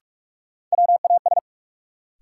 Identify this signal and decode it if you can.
Morse code, heard as a keyed tone, about 43 wpm, 700 Hz. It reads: WAS
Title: Sentences from Top 100 Words - New Word Drill - 45wpm